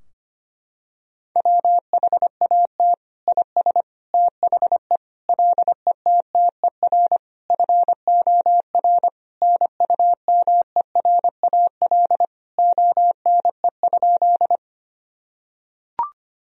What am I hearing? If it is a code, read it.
Morse code, heard as a keyed tone, about 25 wpm, 700 Hz. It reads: WHAT IS THE LETTER FOR NUMERAL ONE?
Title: Sentences from Top 1000 Words - Review Part 2 - 25wpm